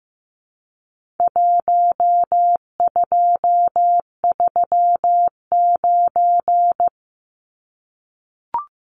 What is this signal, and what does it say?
Morse code, heard as a keyed tone, about 15 wpm, 700 Hz. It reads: 1239
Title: Numbers - 4-Digit 15wpm